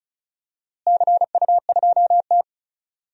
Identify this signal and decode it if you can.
Morse code, heard as a keyed tone, about 34 wpm, 700 Hz. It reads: CU2T
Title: DXCC-100 Call Signs - 4 Characters - Repeated 3x - 35wpm